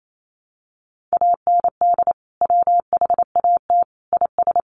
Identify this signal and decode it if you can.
Morse code, heard as a keyed tone, about 28 wpm, 700 Hz. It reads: AND WHAT IS
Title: Sentences from Top 100 Words 28wpm